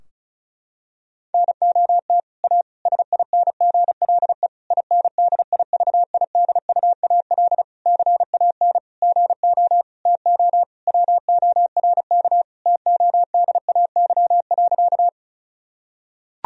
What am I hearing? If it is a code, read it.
Morse code, heard as a keyed tone, about 36 wpm, 700 Hz. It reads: NOT A SINGLE INDIVIDUAL CAN GO TO WORK TODAY.
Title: Sentences from Top 1000 Words - Review Part 4 - 35wpm